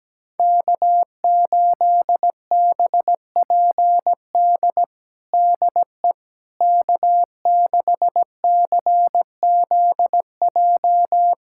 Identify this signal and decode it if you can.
Morse code, heard as a keyed tone, about 17 wpm, 700 Hz. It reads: K8BPD DE K6CZJ
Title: USA Callsign DE Callsign - 5 Characters - 17wpm